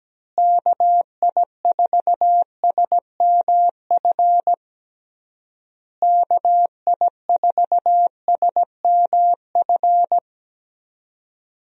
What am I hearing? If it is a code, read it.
Morse code, heard as a keyed tone, about 17 wpm, 700 Hz. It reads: KI4SMF KI4SMF
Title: USA Call Signs - 6 Characters - Repeated 3x - 17wpm